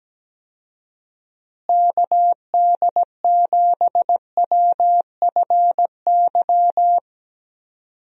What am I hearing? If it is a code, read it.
Morse code, heard as a keyed tone, about 17 wpm, 700 Hz. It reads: KD7WFY
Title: USA Call Signs - 6 Characters - Repeated 2x - 17wpm